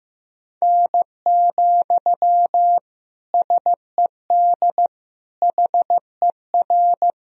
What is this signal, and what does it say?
Morse code, heard as a keyed tone, about 15 wpm, 700 Hz. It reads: N, SED HER
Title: Sentences from Top 300 Words and Abbreviations 15wpm